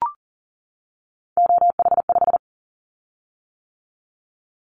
Morse code, heard as a keyed tone, about 38 wpm, 700 Hz. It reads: OH5
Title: Callsign Prefixes for Top 25 DXCC - 40wpm